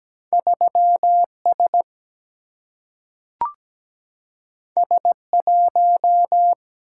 Morse code, heard as a keyed tone, about 17 wpm, 700 Hz. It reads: 3S S1
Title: Single-Letter Single-Number 17wpm